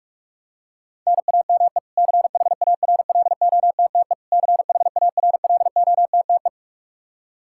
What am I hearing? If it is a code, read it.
Morse code, heard as a keyed tone, about 46 wpm, 700 Hz. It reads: NAME CHARLOTTE CHARLOTTE
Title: Practice QSO - Exchange 1 - Beginning - 45wpm